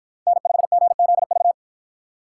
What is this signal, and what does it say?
Morse code, heard as a keyed tone, about 50 wpm, 700 Hz. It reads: N5GZV